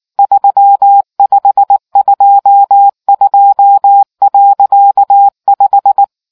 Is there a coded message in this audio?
Yes: Morse code from 0.2 to 6.0 s, heard as a keyed tone, about 19 wpm, 800 Hz. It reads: 3522.5